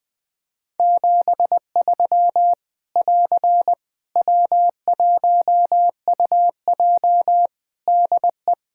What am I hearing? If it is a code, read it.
Morse code, heard as a keyed tone, about 20 wpm, 700 Hz. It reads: 73 <AR> W1UJ DE
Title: Practice QSO - Exchange 4 - Ending - 20wpm